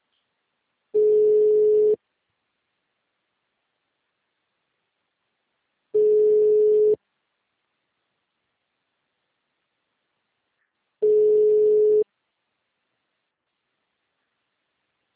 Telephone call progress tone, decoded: ringback tone

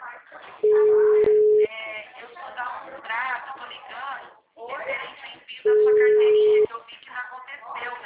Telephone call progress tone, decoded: ringback tone

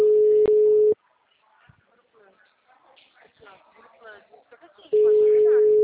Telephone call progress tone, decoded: ringback tone

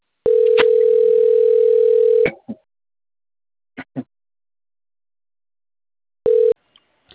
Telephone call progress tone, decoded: ringback tone

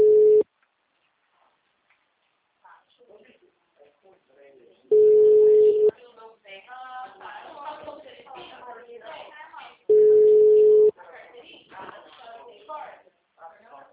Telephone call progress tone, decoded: ringback tone

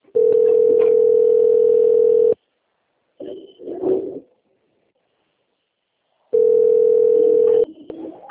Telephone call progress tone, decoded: ringback tone

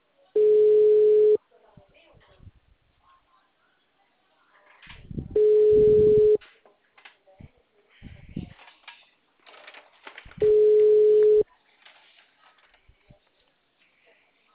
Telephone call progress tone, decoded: ringback tone

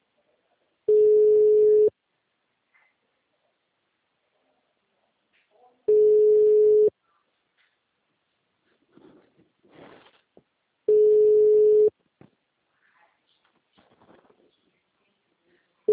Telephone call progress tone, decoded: ringback tone

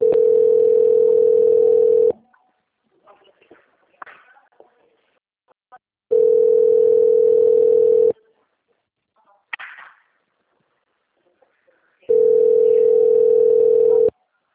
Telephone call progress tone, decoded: ringback tone

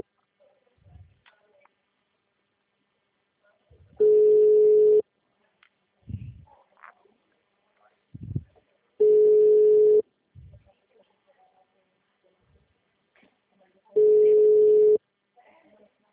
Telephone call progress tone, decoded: ringback tone

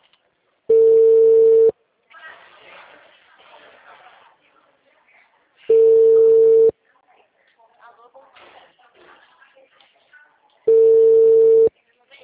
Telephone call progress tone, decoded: ringback tone

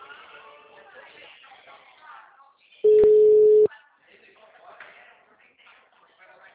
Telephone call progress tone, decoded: ringback tone

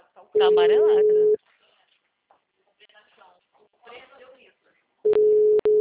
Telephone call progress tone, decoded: ringback tone

